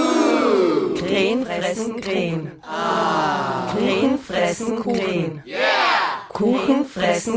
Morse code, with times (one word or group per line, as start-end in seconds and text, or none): none